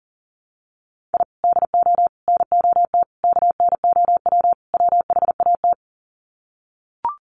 Morse code, heard as a keyed tone, about 40 wpm, 700 Hz. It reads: I DO NOT KNOW WHAT
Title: Sets of 5 Words 40wpm